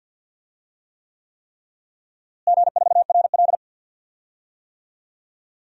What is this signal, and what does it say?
Morse code, heard as a keyed tone, about 47 wpm, 700 Hz. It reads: G4RL
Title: DXCC-100 Call Signs - 4 Characters - Repeated 3x - 50wpm